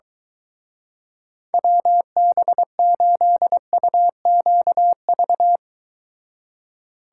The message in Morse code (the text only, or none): WB8UQV